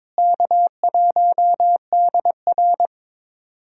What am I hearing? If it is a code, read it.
Morse code, heard as a keyed tone, about 22 wpm, 700 Hz. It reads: K1DR
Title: USA Call Signs - 4 Characters - Repeated 2x - 22wpm